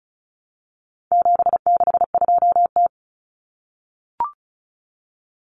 Morse code, heard as a keyed tone, about 34 wpm, 700 Hz. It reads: 762T
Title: Common Cut Numbers - 4-Digit 35wpm